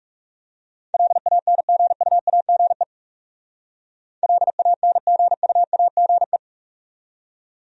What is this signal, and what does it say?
Morse code, heard as a keyed tone, about 38 wpm, 700 Hz. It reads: LANGUAGE LANGUAGE